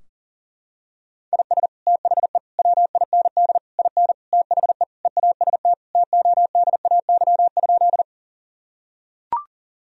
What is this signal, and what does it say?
Morse code, heard as a keyed tone, about 37 wpm, 700 Hz. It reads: IS THE WIND IN THE EAST TODAY?
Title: Sentences from Top 1000 Words - Review Part 3 - 40wpm